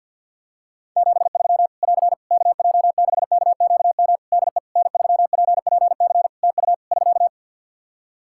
Morse code, heard as a keyed tone, about 51 wpm, 700 Hz. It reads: 73 <AR> KJ6KQM DE N3PPC TU <SK>
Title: K1USN SST QSO Practice 50wpm